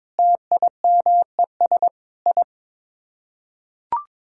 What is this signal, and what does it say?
Morse code, heard as a keyed tone, about 22 wpm, 700 Hz. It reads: TIMES I
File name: practice / 22wpm / Sets of 8 Words - 22wpm